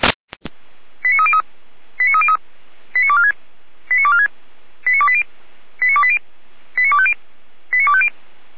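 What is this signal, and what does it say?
Five-tone selective call (ZVEI), heard as quick runs of five tones, tones of about 70 ms. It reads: 89303 89303 89246 89246 89280 89280 89250 89250
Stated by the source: ZVEI five-tone selective call received on an SDR, from the Feuerwehr Orlamünde fire brigade audio server.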